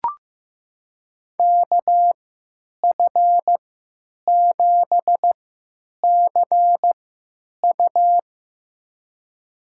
Morse code, heard as a keyed tone, about 15 wpm, spaced as for 9 wpm, 700 Hz. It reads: KF7CU
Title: USA Call Signs - 5 Characters - 3x Character Spacing - 15wpm